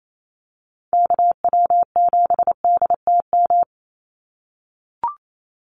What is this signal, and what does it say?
Morse code, heard as a keyed tone, about 28 wpm, 700 Hz. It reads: KW7DTM
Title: USA Call Signs 28wpm